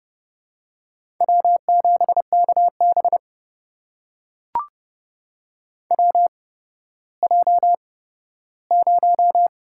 Morse code, heard as a keyed tone, about 31 wpm, 700 Hz. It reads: W 7 K B W J 0
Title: USA Call Signs - 4 Characters - 8x Character Spacing - 30wpm